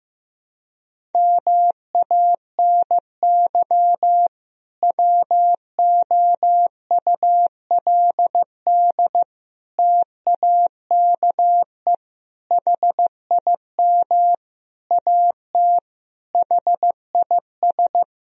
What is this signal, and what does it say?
Morse code, heard as a keyed tone, about 16 wpm, 700 Hz. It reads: MANY WOULD TAKE HIM AT HIS
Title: Sentences from Top 1000 Words - Review Part 1 - 15wpm